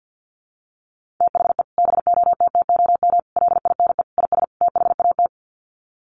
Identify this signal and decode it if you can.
Morse code, heard as a keyed tone, about 50 wpm, 700 Hz. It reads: THE BOTTOM LINE IS THAT
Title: Sets of 5 Words Encore 50wpm